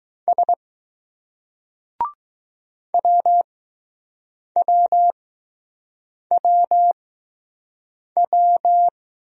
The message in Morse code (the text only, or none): S WWWW